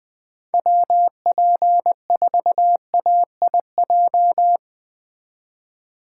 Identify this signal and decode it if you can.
Morse code, heard as a keyed tone, about 20 wpm, 700 Hz. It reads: WP4AIJ
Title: Speed Racing - USA 2x3 Call Signs 17wpm